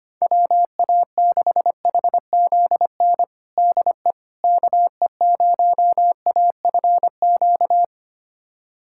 Morse code, heard as a keyed tone, about 25 wpm, 700 Hz. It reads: WA6HZN DE KE0AFQ
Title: USA Callsign DE Callsign - 6 Characters - 25wpm